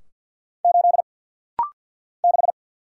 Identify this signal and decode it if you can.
Morse code, heard as a keyed tone, about 50 wpm, 700 Hz. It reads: 8 6
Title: Single Letter-Number Alt - Rapid-Fire 50wpm